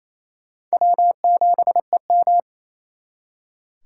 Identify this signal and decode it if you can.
Morse code, heard as a keyed tone, about 34 wpm, 700 Hz. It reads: W7EM
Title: Speed Racing - USA 1x2 Call Signs 28wpm